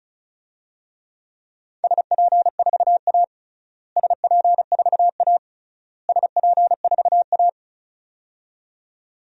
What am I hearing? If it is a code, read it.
Morse code, heard as a keyed tone, about 34 wpm, 700 Hz. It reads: SP4A SP4A SP4A
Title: DXCC-100 Call Signs - 4 Characters - Repeated 3x - 35wpm